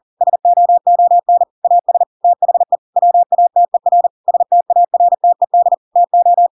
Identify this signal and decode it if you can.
Morse code, heard as a keyed tone, about 40 wpm, 700 Hz. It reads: SOON AS THE WATER STARTED TO
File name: Pinocchio / Pinocchio ch5 40 40 ik1hge